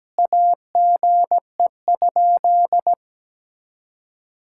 Morse code, heard as a keyed tone, about 17 wpm, 700 Hz. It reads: AGE?
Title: Sentences from Top 1600 Words 17wpm